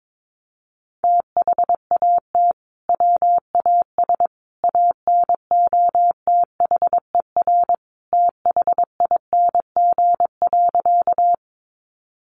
Morse code, heard as a keyed tone, about 22 wpm, 700 Hz. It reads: THAT WAS ANOTHER THING.
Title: Sentences from Top 600 Words 22wpm